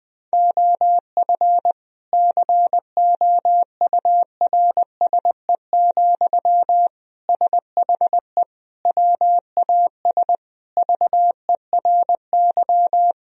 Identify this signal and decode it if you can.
Morse code, heard as a keyed tone, about 20 wpm, 700 Hz. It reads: OF COURSE, SHE WAS VERY
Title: Sentences from Top 1000 Words - Review Part 4 - 20wpm